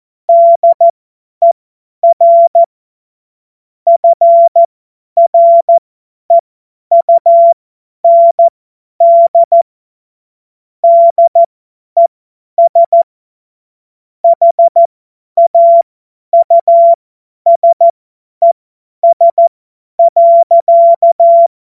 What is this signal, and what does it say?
Morse code, heard as a keyed tone, about 14 wpm, spaced as for 10 wpm, 680 Hz. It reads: DER FREUND DES HAUSES.
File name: Morsekurs / Witze / Witz 08. Februar 2022 50 F70